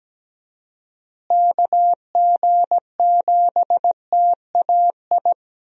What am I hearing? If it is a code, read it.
Morse code, heard as a keyed tone, about 17 wpm, 700 Hz. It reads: KG7TAI
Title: USA Call Signs - 6 Characters - Repeated 3x - 17wpm